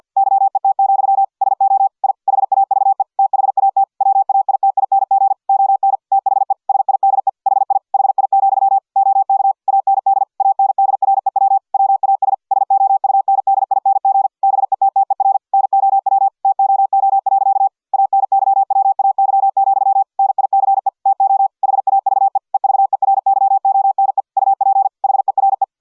Morse code, heard as a keyed tone, about 50 wpm, 780 Hz. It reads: YET, SO I HAVE THAT WAITING ON THE SIDE HI HI, OK AND ANDREW WAS SOUNDING BETTER NOW TOO. ANYWAY, NICE TO HAVE EVERYONE UP HERE